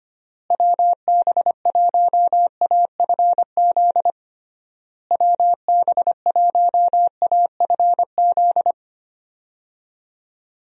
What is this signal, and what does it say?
Morse code, heard as a keyed tone, about 25 wpm, 700 Hz. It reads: WB1AFZ WB1AFZ
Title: USA Call Signs - 6 Characters - Repeated 3x - 25wpm